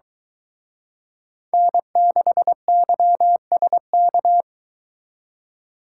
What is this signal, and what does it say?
Morse code, heard as a keyed tone, about 22 wpm, 700 Hz. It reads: N6YSK